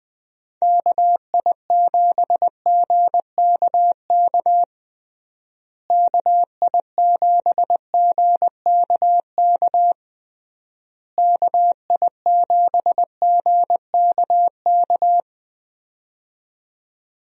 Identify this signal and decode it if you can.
Morse code, heard as a keyed tone, about 21 wpm, 700 Hz. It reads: KI7GKK KI7GKK KI7GKK
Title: USA Call Signs - 6 Characters - Repeated 3x - 20wpm